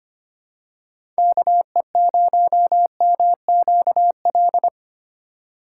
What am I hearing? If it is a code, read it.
Morse code, heard as a keyed tone, about 25 wpm, 700 Hz. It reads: KE0MQL
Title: USA Call Signs - 6 Characters - 2x Character Spacing - 25wpm